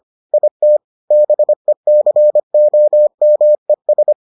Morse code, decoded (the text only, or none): IT BECOMES